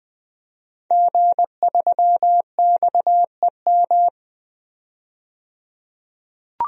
Morse code, heard as a keyed tone, about 20 wpm, 700 Hz. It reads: G3XEM
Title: DXCC-100 Call Signs - 5 Characters - 20wpm